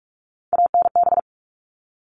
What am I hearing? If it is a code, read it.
Morse code, heard as a keyed tone, about 44 wpm, 700 Hz. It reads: ANB
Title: All Cut Numbers - 3-Digit 45wpm